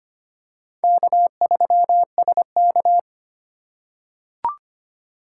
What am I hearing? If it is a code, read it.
Morse code, heard as a keyed tone, about 25 wpm, 700 Hz. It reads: K3SK
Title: USA Call Signs - 4 Characters - 2x Character Spacing - 25wpm